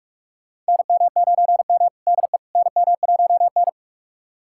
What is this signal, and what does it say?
Morse code, heard as a keyed tone, about 46 wpm, 700 Hz. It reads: NM9M DE NM1N
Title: USA Callsign DE Callsign - 4 Characters - 45wpm